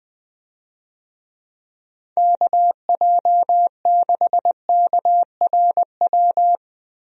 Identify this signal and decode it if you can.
Morse code, heard as a keyed tone, about 20 wpm, 700 Hz. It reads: KJ6KRW
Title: USA Call Signs - 6 Characters - Repeated 3x - 20wpm